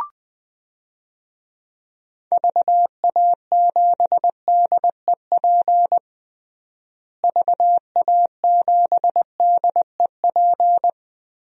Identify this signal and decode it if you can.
Morse code, heard as a keyed tone, about 20 wpm, 700 Hz. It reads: VA7DEP VA7DEP
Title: DXCC-100 Call Signs - 6 Characters - Repeated 3x - 20wpm